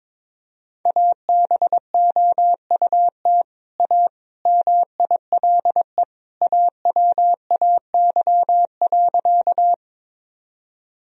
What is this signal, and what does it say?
Morse code, heard as a keyed tone, about 22 wpm, 700 Hz. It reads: ABOUT A MILE AWAY.
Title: Sentences from Top 1000 Words - Review Part 1 - 22wpm